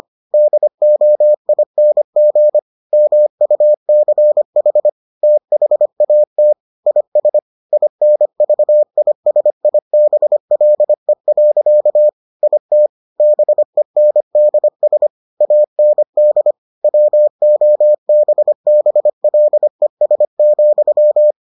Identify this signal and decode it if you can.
Morse code, heard as a keyed tone, about 25 wpm, 600 Hz. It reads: DOING MUCH THAT IS INVISIBLE. IT BENDS AND WOBBLES,